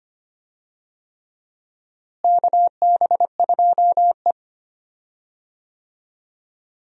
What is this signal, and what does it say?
Morse code, heard as a keyed tone, about 25 wpm, 700 Hz. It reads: KB2E
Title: USA Call Signs - 4 Characters - Repeated 3x - 25wpm